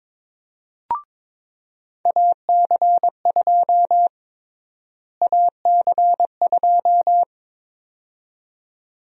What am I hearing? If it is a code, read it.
Morse code, heard as a keyed tone, about 21 wpm, 700 Hz. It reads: AC2 AC2
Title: USA Call Sign Prefixes - Repeated 2x - 22wpm